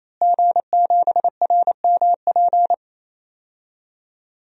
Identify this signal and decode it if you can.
Morse code, heard as a keyed tone, about 27 wpm, 700 Hz. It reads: G7RMP
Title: DXCC-100 Call Signs - 5 Characters - Repeated 3x - 28wpm